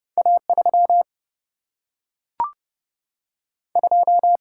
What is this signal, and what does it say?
Morse code, heard as a keyed tone, about 30 wpm, 700 Hz. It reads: A 3 2
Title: Single-Letter Single-Number 30wpm